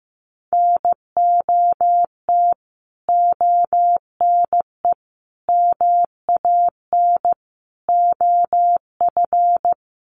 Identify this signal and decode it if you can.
Morse code, heard as a keyed tone, about 14 wpm, 700 Hz. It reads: NOT ONE MAN OF
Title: Sentences from Top 1500 Words 15wpm